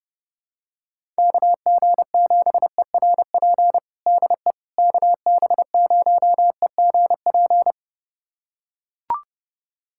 Morse code, heard as a keyed tone, about 30 wpm, 700 Hz. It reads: KG7ERP DE KB0EGP
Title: USA Callsign DE Callsign 30wpm